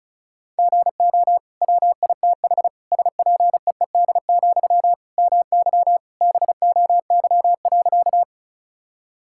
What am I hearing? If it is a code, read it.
Morse code, heard as a keyed tone, about 36 wpm, 700 Hz. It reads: GO WITH SPEED, MY BOY.
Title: Sentences from Top 1000 Words - Review Part 5 - 35wpm